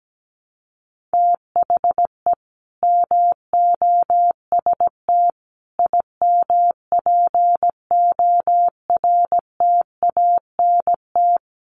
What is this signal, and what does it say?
Morse code, heard as a keyed tone, about 17 wpm, 700 Hz. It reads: THE MOST IMPORTANT
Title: Sets of 5 Words Encore 17wpm